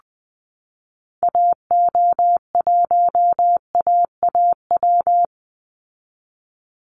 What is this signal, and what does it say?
Morse code, heard as a keyed tone, about 20 wpm, 700 Hz. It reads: AO1AAW
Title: Call Signs Repeated 3x - 20wpm